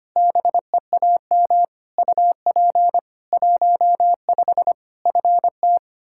Morse code, heard as a keyed tone, about 25 wpm, 700 Hz. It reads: BEAM UP 15 FT
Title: K1USN SST QSO Practice 25wpm